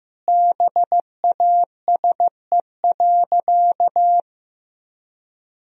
Morse code, heard as a keyed tone, about 15 wpm, 700 Hz. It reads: BASE.
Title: Sentences from Top 1000 Words - Review Part 1 - 15wpm